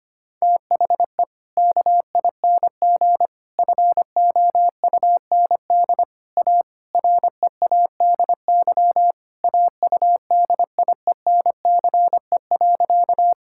Morse code, heard as a keyed tone, about 25 wpm, 700 Hz. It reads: THE KING FOUND A READY AUDIENCE.
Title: Sentences from Top 1000 Words - Review Part 5 - 25wpm